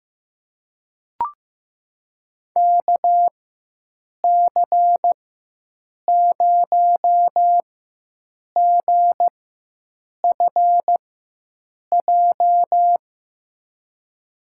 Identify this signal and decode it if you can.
Morse code, heard as a keyed tone, about 15 wpm, spaced as for 7 wpm, 700 Hz. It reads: KC0GFJ
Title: USA Call Signs - 6 Characters - 4x Character Spacing - 15wpm